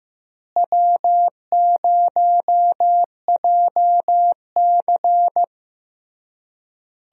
Morse code, heard as a keyed tone, about 15 wpm, 700 Hz. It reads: W0JC
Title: USA Call Signs - 4 Characters - Repeated 3x - 15wpm